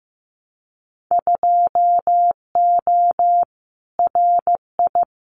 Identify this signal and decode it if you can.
Morse code, heard as a keyed tone, about 15 wpm, 700 Hz. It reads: 2O RI
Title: Winter Field Day Exchanges 15wpm